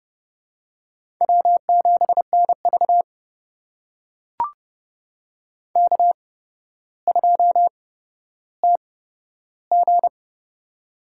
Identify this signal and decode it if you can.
Morse code, heard as a keyed tone, about 29 wpm, spaced as for 13 wpm, 700 Hz. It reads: W7NV K2TG
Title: USA Call Signs - 4 Characters - 8x Character Spacing - 30wpm